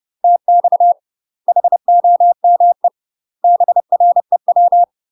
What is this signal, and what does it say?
Morse code, heard as a keyed tone, about 30 wpm, 700 Hz. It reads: TX HOME BREW
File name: UK MASTER QSOs TRAINING / 30WPM / UK MASTER QSOs 0001